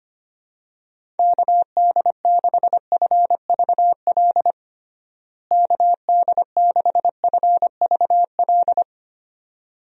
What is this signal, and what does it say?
Morse code, heard as a keyed tone, about 25 wpm, 700 Hz. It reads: KD6FVL KD6FVL
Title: USA Call Signs - 6 Characters - Repeated 2x - 25wpm